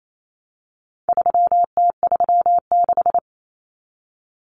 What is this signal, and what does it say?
Morse code, heard as a keyed tone, about 28 wpm, 700 Hz. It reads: 3T36
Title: Common Cut Numbers - 4-Digit 28wpm